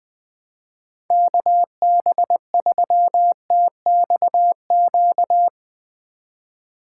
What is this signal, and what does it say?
Morse code, heard as a keyed tone, about 20 wpm, 700 Hz. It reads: KB3TXQ
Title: Speed Racing - USA 2x3 Call Signs 17wpm